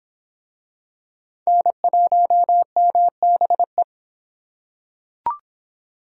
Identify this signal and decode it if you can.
Morse code, heard as a keyed tone, about 25 wpm, 700 Hz. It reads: N1MBE